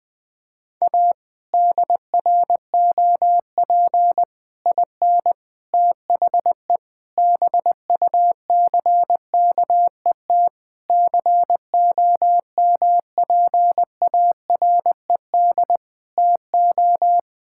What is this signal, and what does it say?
Morse code, heard as a keyed tone, about 21 wpm, 700 Hz. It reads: A DROP IN THE BUCKET COMPARED TO
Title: Sets of 8 Words - 20wpm